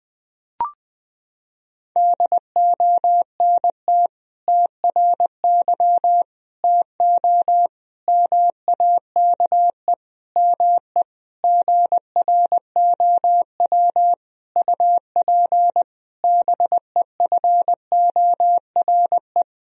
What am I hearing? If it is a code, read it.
Morse code, heard as a keyed tone, about 20 wpm, 700 Hz. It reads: DONT TRY TO MAKE ME GROW UP BEFORE
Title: Sentences from Top 1000 Words - Review Part 1 - 20wpm